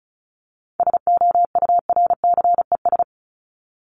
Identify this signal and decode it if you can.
Morse code, heard as a keyed tone, about 34 wpm, 700 Hz. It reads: SOURCES
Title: ES:S Suffix - Top 100 Words with Base Words 35wpm